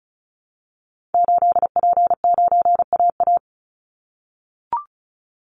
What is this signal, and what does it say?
Morse code, heard as a keyed tone, about 34 wpm, 700 Hz. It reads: 8P9AA
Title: Call Signs - 35wpm